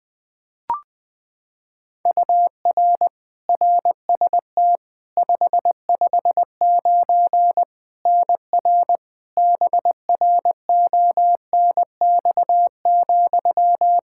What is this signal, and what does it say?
Morse code, heard as a keyed tone, about 21 wpm, 700 Hz. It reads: UR RST 559 NR BRONX,
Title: Practice QSO - Exchange 1 - Beginning - 20wpm